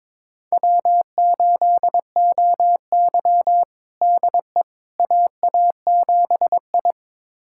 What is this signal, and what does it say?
Morse code, heard as a keyed tone, about 22 wpm, 700 Hz. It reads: W8OY DE AA7I
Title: USA Callsign DE Callsign - 4 Characters - 22wpm